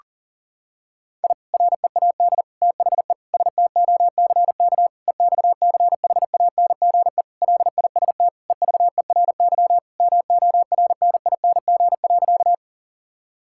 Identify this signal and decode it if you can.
Morse code, heard as a keyed tone, about 37 wpm, 700 Hz. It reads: I READ THE STOCK EXCHANGE LIST EVERY MORNING.
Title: Sentences from Top 1000 Words - Review Part 6 - 40wpm